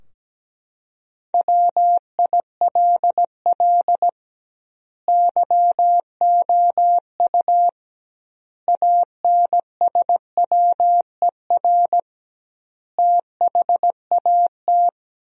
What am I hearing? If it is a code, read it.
Morse code, heard as a keyed tone, about 17 wpm, 700 Hz. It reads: WILL YOU ANSWER THAT
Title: 2x Word-Spacing - Sentences from Top 400 Words - 17wpm